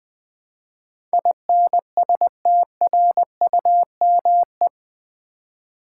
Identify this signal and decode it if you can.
Morse code, heard as a keyed tone, about 20 wpm, 700 Hz. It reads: INSTRUME
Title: Build A Word - 4 Letters At A Time - 10-Letter Words - 20wpm